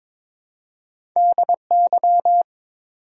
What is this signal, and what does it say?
Morse code, heard as a keyed tone, about 22 wpm, 700 Hz. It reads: DY
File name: practice / 22wpm / Two Letter-Number Combinations - 22wpm